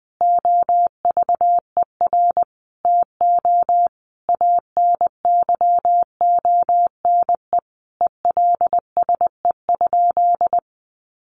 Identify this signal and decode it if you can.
Morse code, heard as a keyed tone, about 20 wpm, 700 Hz. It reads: OVER TO ANYONE ELSE?
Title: Sentences from Top 900 Words 20wpm